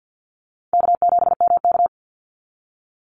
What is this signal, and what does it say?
Morse code, heard as a keyed tone, about 50 wpm, 700 Hz. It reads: K7MK
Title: NA SOTA Activator Callsigns 50wpm